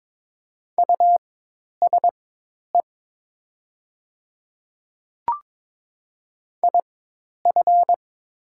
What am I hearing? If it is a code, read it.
Morse code, heard as a keyed tone, about 22 wpm, spaced as for 10 wpm, 700 Hz. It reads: USE IF
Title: 4x Character Spacing - Top 100 Words - 22wpm